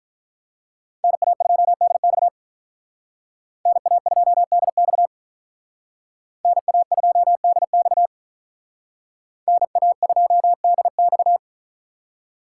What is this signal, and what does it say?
Morse code, heard as a keyed tone, about 43 wpm, 700 Hz. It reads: NA2DX NA2DX NA2DX NA2DX
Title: Speed Racing - USA Call Signs 35wpm